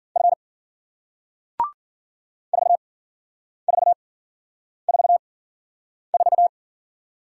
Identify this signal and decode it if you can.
Morse code, heard as a keyed tone, about 54 wpm, spaced as for 9 wpm, 700 Hz. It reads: F 4444